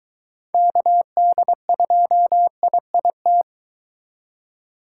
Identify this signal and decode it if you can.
Morse code, heard as a keyed tone, about 23 wpm, 700 Hz. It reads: KD2IIT